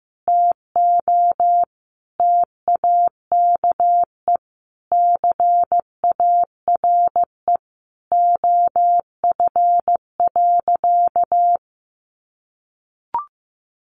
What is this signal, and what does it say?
Morse code, heard as a keyed tone, about 15 wpm, 700 Hz. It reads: TO TAKE CARE OF.
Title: Easy Phrases 15wpm